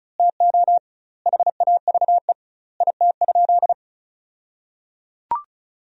Morse code, heard as a keyed tone, about 36 wpm, 700 Hz. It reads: TO HAVE IT?
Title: 2x Word-Spacing - Sentences from Top 100 Words - 35wpm